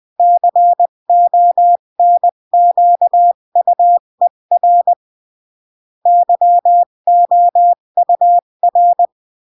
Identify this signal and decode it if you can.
Morse code, heard as a keyed tone, about 20 wpm, 700 Hz. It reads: CONQUER YOUR
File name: CW / jodorowsky / 82 commandments 20-15